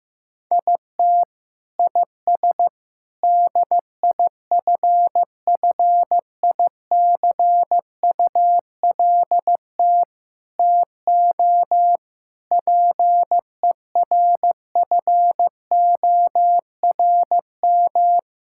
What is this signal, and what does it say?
Morse code, heard as a keyed tone, about 15 wpm, 700 Hz. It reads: IT IS DIFFICULT TO PERFORM
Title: Sentences from Top 1000 Words - Review Part 6 - 15wpm